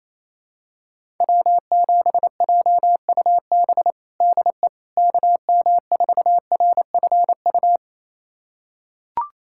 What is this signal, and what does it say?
Morse code, heard as a keyed tone, about 28 wpm, 700 Hz. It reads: W7JUB DE KM4RFU
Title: USA Callsign DE Callsign 28wpm